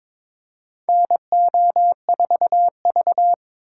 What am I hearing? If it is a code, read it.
Morse code, heard as a keyed tone, about 22 wpm, 700 Hz. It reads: NO4V
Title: USA Call Signs - 4 Characters - 2x Character Spacing - 22wpm